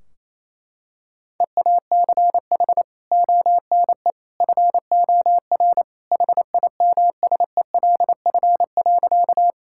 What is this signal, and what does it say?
Morse code, heard as a keyed tone, about 28 wpm, 700 Hz. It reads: EACH ONE FOR HIMSELF.
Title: Sentences from Top 1600 Words 28wpm